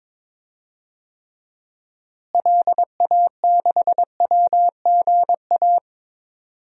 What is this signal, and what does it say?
Morse code, heard as a keyed tone, about 21 wpm, 700 Hz. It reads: LA6WGA
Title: DXCC-100 Call Signs - 6 Characters - Repeated 2x - 22wpm